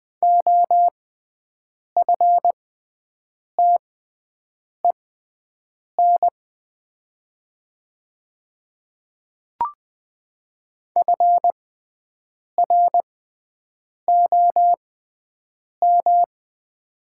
Morse code, heard as a keyed tone, about 20 wpm, spaced as for 7 wpm, 700 Hz. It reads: OFTEN FROM
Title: 6x Character Spacing - Top 500 Words - 20wpm